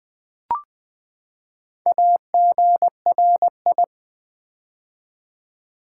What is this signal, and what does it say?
Morse code, heard as a keyed tone, about 20 wpm, 700 Hz. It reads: AGRI
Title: Build A Word - 4 Letters At A Time - 12-Letter Words - 20wpm